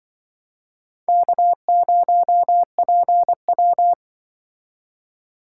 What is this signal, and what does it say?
Morse code, heard as a keyed tone, about 24 wpm, 700 Hz. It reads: K0PW
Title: Speed Racing - USA 1x2 Call Signs 20wpm